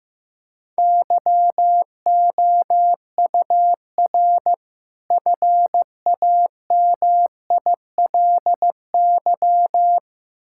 Morse code, heard as a keyed tone, about 16 wpm, 700 Hz. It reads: YOUR FAMILY
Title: Sentences from Top 1600 Words 15wpm